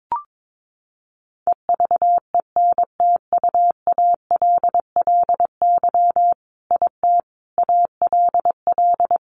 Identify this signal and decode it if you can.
Morse code, heard as a keyed tone, about 22 wpm, 700 Hz. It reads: EVENTUALLY IT ALL
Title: A Sound Of Thunder 22wpm